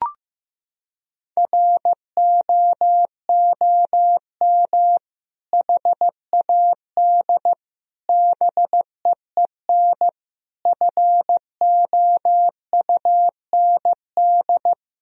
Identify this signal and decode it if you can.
Morse code, heard as a keyed tone, about 15 wpm, 700 Hz. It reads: ROOM HAD BEEN FOUND
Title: Sentences from Top 1000 Words - Review Part 3 - 15wpm